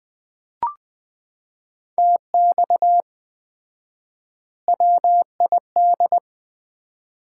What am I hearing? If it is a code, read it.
Morse code, heard as a keyed tone, about 21 wpm, 700 Hz. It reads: TX WID
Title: Common QSO Phrases - 4x Word-Spacing - 20wpm